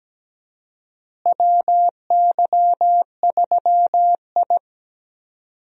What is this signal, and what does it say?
Morse code, heard as a keyed tone, about 17 wpm, 700 Hz. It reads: WY3I
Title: USA Call Signs - 4 Characters - Repeated 3x - 17wpm